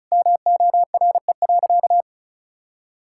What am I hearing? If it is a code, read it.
Morse code, heard as a keyed tone, about 36 wpm, 700 Hz. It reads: MORE.
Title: Sentences from Top 1000 Words - Review Part 5 - 35wpm